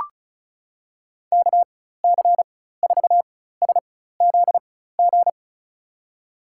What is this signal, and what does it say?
Morse code, heard as a keyed tone, about 34 wpm, spaced as for 16 wpm, 700 Hz. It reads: KC4SZG
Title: USA Call Signs - 6 Characters - 4x Character Spacing - 35wpm